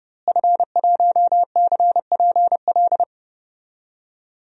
Morse code, heard as a keyed tone, about 30 wpm, 700 Hz. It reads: F1CPL
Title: DXCC-100 Call Signs - 5 Characters - Repeated 2x - 30wpm